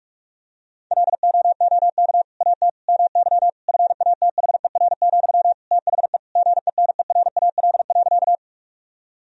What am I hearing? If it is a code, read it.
Morse code, heard as a keyed tone, about 46 wpm, 700 Hz. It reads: LOOK AT MY FATHER, THE GENERAL.
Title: Sentences from Top 1000 Words - Review Part 3 - 45wpm